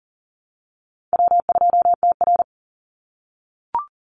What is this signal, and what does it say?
Morse code, heard as a keyed tone, about 40 wpm, 700 Hz. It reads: W2TR
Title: Call Signs - 40wpm